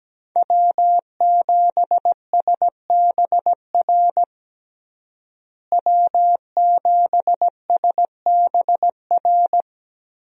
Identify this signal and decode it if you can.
Morse code, heard as a keyed tone, about 17 wpm, 700 Hz. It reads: W7SBR W7SBR
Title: USA Call Signs - 5 Characters - Repeated 3x - 17wpm